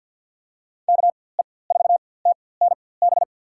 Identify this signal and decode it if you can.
Morse code, heard as a keyed tone, about 51 wpm, spaced as for 23 wpm, 700 Hz. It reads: KE4TNB
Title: USA Call Signs - 6 Characters - 4x Character Spacing - 50wpm